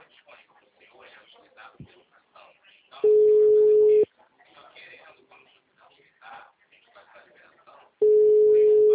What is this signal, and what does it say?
Ringback tone (Europe, 425 Hz), with speech.